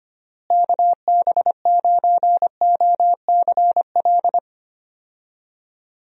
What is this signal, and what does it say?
Morse code, heard as a keyed tone, about 25 wpm, 700 Hz. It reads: KB9OCL
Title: USA Call Signs - 6 Characters - Repeated 3x - 25wpm